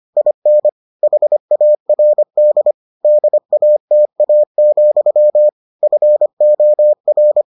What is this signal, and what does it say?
Morse code, heard as a keyed tone, about 25 wpm, 600 Hz. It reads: IN HARD DATA, FOR